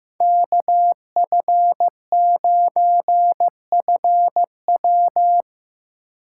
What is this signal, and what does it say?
Morse code, heard as a keyed tone, about 15 wpm, 700 Hz. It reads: KF9FW
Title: USA Call Signs - 2x Character Spacing - 15wpm